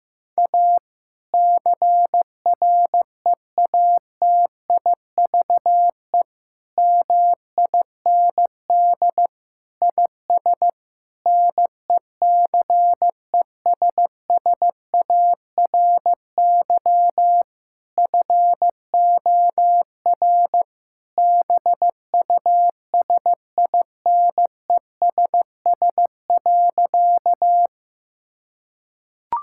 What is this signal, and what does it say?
Morse code, heard as a keyed tone, about 15 wpm, 700 Hz. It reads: A CREATIVE MIND IS NECESSARY FOR BUSINESS.
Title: Sentences from Top 1000 Words - Review Part 5 - 15wpm